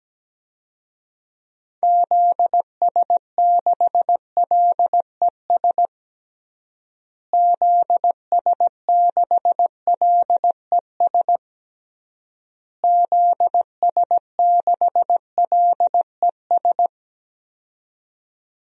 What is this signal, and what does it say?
Morse code, heard as a keyed tone, about 17 wpm, 700 Hz. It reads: ZS6LES ZS6LES ZS6LES
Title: DXCC-100 Call Signs - 6 Characters - Repeated 3x - 17wpm